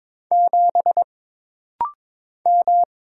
Morse code, heard as a keyed tone, about 23 wpm, 700 Hz. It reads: 7 M
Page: This rapid-fire timing is geared to push your ICR (Instant Character Recognition) skills.